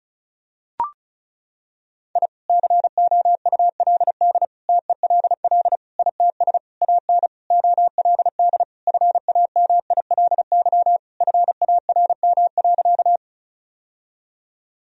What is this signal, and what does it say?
Morse code, heard as a keyed tone, about 34 wpm, 700 Hz. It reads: I COULD TELL ITS AN OLD FAMILY FARM.
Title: Sentences from Top 1000 Words - Review Part 2 - 35wpm